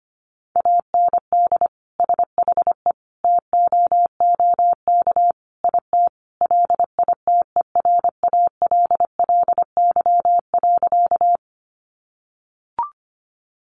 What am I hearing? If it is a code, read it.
Morse code, heard as a keyed tone, about 25 wpm, 700 Hz. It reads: AND SHE TOOK IT LITERALLY.
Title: Sentences from Top 1400 Words 25wpm